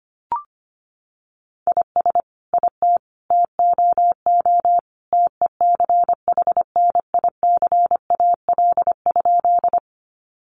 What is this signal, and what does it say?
Morse code, heard as a keyed tone, about 25 wpm, 700 Hz. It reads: IS IT TOO TECHNICAL?
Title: Sentences from Top 1100 Words 25wpm